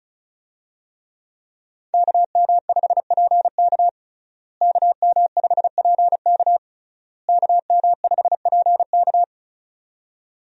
Morse code, heard as a keyed tone, about 36 wpm, 700 Hz. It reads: KM5PK KM5PK KM5PK
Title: USA Call Signs - 5 Characters - Repeated 3x - 35wpm